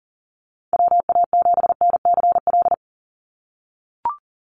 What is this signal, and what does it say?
Morse code, heard as a keyed tone, about 40 wpm, 700 Hz. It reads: WA7NCL
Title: NA SOTA Activator Callsigns 40wpm